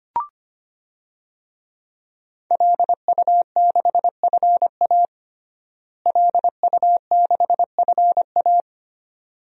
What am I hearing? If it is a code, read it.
Morse code, heard as a keyed tone, about 25 wpm, 700 Hz. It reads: LU6FA LU6FA
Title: DXCC-100 Call Signs - 5 Characters - Repeated 2x - 25wpm